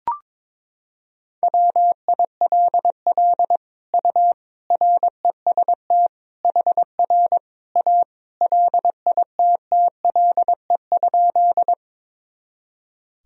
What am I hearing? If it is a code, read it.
Morse code, heard as a keyed tone, about 22 wpm, 700 Hz. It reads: WILL U REST HR A LITTLE?
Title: Sentences from Top 500 Words and Abbreviations 22wpm